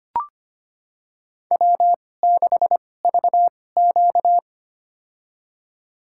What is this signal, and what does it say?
Morse code, heard as a keyed tone, about 25 wpm, spaced as for 18 wpm, 700 Hz. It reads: W6VQ
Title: USA Call Signs - 4 Characters - 2x Character Spacing - 25wpm